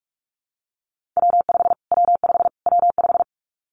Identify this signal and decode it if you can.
Morse code, heard as a keyed tone, about 44 wpm, 700 Hz. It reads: W5 W5 W5
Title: Callsign Prefixes for Top 25 DXCC - Repeated 3x - 45wpm